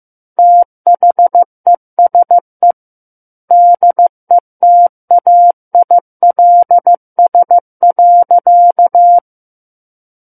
Morse code, heard as a keyed tone, about 15 wpm, 700 Hz. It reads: THESE DETAILS.
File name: Chapter 6 - 15 WPM